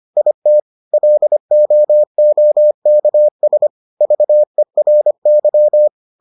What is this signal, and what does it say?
Morse code, heard as a keyed tone, about 25 wpm, 600 Hz. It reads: IT LOOKS VERY